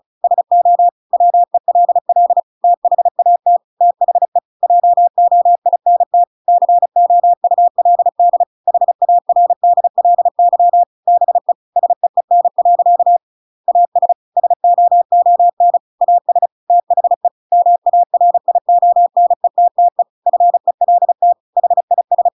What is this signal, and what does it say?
Morse code, heard as a keyed tone, about 34 wpm, 700 Hz. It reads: SO WELL THAT THE JOINT COULD HARDLY BE SEEN. AS SOON AS THE MARIONETTE FELT HIS
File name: Pinocchio / Pinocchio ch8 35 35 ik1hge